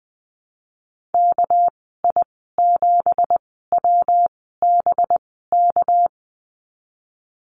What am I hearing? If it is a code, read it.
Morse code, heard as a keyed tone, about 20 wpm, spaced as for 15 wpm, 700 Hz. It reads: KI7WBK
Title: 207 Lesson 33g - Letter j - Call Signs 20wpm